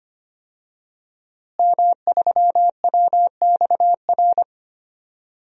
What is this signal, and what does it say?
Morse code, heard as a keyed tone, about 25 wpm, 700 Hz. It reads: M3WXR
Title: DXCC-100 Call Signs - 5 Characters - Repeated 3x - 25wpm